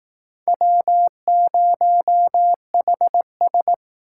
Morse code, heard as a keyed tone, about 19 wpm, 700 Hz. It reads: W0HS